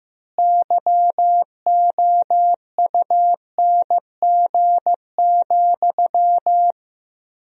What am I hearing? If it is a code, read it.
Morse code, heard as a keyed tone, about 15 wpm, 700 Hz. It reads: YOUNG,
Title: 3x Word-Spacing - Sentences from Top 400 Words - 15wpm